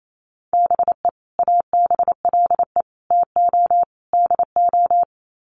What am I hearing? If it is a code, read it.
Morse code, heard as a keyed tone, about 28 wpm, 700 Hz. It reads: BE ABLE TO DO
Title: Sets of 4 Words 28wpm